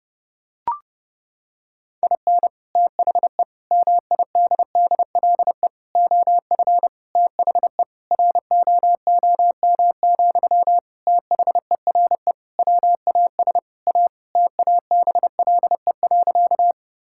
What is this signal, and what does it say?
Morse code, heard as a keyed tone, about 30 wpm, 700 Hz. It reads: IN THE MIDDLE OF THE ROOM, THERE WAS A TABLE.
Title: Sentences from Top 1000 Words - Review Part 6 - 30wpm